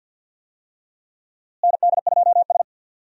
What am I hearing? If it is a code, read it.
Morse code, heard as a keyed tone, about 50 wpm, 700 Hz. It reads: ND2S
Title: USA Call Signs - 4 Characters - Repeated 3x - 50wpm